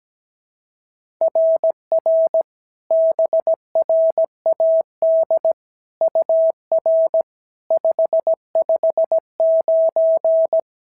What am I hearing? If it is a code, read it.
Morse code, heard as a keyed tone, about 16 wpm, 650 Hz. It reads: RR BRAD UR 559
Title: SOTA QSO Practice 17wpm